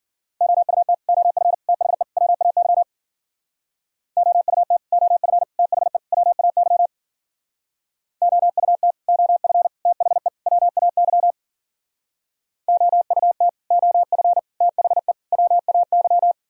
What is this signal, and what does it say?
Morse code, heard as a keyed tone, about 53 wpm, 700 Hz. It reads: OUT OF THE WAY OUT OF THE WAY OUT OF THE WAY OUT OF THE WAY